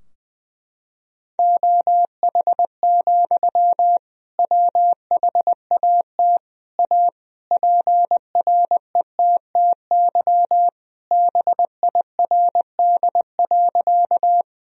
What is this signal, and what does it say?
Morse code, heard as a keyed tone, about 21 wpm, 700 Hz. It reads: OH, WHAT A PRETTY BIRD.
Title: Sentences from Top 1000 Words - Review Part 2 - 20wpm